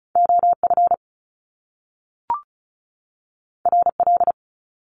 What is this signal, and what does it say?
Morse code, heard as a keyed tone, about 36 wpm, 700 Hz. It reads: OF RL
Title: Top Two Letter Combinations in Words 35wpm